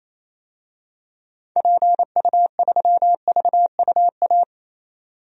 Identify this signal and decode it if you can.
Morse code, heard as a keyed tone, about 27 wpm, 700 Hz. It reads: PU3VUA
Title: DXCC-100 Call Signs - 6 Characters - 28wpm